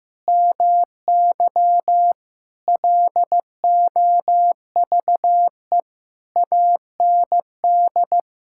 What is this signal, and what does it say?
Morse code, heard as a keyed tone, about 15 wpm, 700 Hz. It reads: MY LOVE AND
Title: Sentences from Top 1000 Words - Review Part 6 - 15wpm